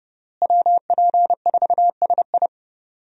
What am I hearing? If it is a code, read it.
Morse code, heard as a keyed tone, about 30 wpm, 700 Hz. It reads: WP4SI
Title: USA Call Signs - 5 Characters - Repeated 3x - 30wpm